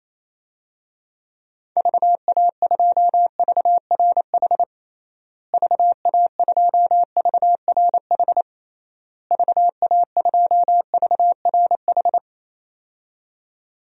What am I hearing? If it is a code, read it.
Morse code, heard as a keyed tone, about 28 wpm, 700 Hz. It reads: VA2VRH VA2VRH VA2VRH
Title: DXCC-100 Call Signs - 6 Characters - Repeated 3x - 28wpm